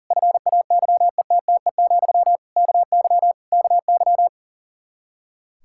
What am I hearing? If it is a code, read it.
Morse code, heard as a keyed tone, about 40 wpm, 700 Hz. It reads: FAYETTE, KY KY
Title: Practice QSO - Exchange 1 - Beginning - 40wpm